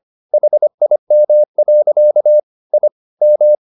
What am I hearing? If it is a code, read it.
Morse code, heard as a keyed tone, about 25 wpm, 600 Hz. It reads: HIM. I M